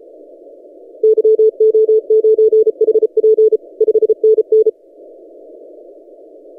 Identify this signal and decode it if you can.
Morse code, heard as a keyed tone, about 33 wpm, 440 Hz. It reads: YO9HP 5NN
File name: tx6g-austral-is-12cw